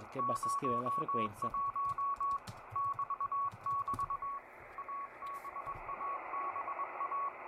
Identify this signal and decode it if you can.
Morse code, heard as a keyed tone, about 21 wpm, 1120 Hz, with speech and radio noise. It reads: 01 <BT><BT> ABABA